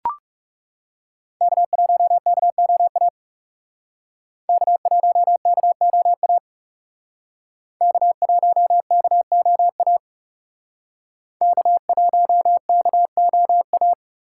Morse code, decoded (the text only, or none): K1KOA K1KOA K1KOA K1KOA